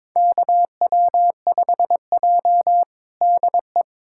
Morse code, heard as a keyed tone, about 22 wpm, 700 Hz. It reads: KW5J DE
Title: USA Callsign DE Callsign - 4 Characters - 22wpm